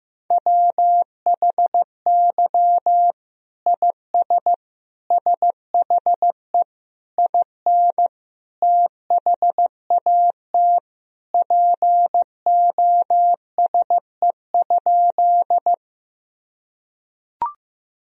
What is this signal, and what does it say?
Morse code, heard as a keyed tone, about 16 wpm, 700 Hz. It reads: WHY IS SHE IN THAT POSE?
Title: Sentences from Top 1000 Words - Review Part 2 - 15wpm